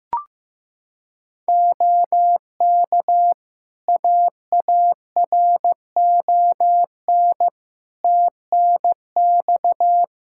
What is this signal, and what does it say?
Morse code, heard as a keyed tone, about 15 wpm, 700 Hz. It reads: OK AARON TNX
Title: K1USN SST QSO Practice 15wpm